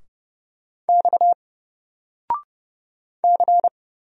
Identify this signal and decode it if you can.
Morse code, heard as a keyed tone, about 31 wpm, 700 Hz. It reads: X C
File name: Single-Letter-Number-30wpm